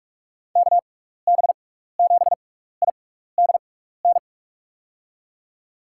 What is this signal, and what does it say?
Morse code, heard as a keyed tone, about 46 wpm, spaced as for 16 wpm, 700 Hz. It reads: KB7IDN
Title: USA Call Signs - 6 Characters - 6x Character Spacing - 45wpm